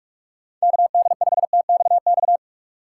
Morse code, heard as a keyed tone, about 47 wpm, 700 Hz. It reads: KD5TXX
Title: K1USN SST QSO Practice - 45wpm